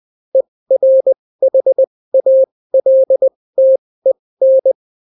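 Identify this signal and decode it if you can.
Morse code, heard as a keyed tone, about 20 wpm, spaced as for 16 wpm, 530 Hz. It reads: ERHALTEN
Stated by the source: Daily Morse weather and top stories at 16 wpm, clean without QRN.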